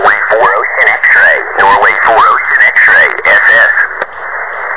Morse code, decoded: none